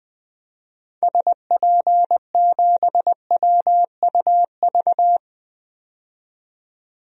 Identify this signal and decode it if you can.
Morse code, heard as a keyed tone, about 20 wpm, 700 Hz. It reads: SP7WUV
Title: DXCC-100 Call Signs - 6 Characters - 20wpm